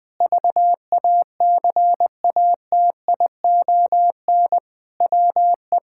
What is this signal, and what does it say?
Morse code, heard as a keyed tone, about 21 wpm, 700 Hz. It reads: VACATION WE
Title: Sentences with 6-Word Phrases - 20wpm